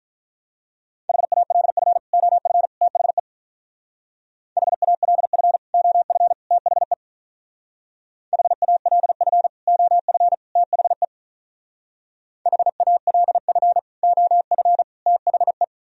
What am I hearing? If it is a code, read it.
Morse code, heard as a keyed tone, about 46 wpm, 700 Hz. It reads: HALF OF THE HALF OF THE HALF OF THE HALF OF THE